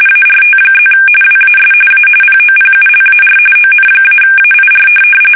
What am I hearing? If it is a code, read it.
RTTY, about 45.45 baud. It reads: ART+ WAS A POPULAR ON-AIR
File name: fsk850 / teletype-RTTY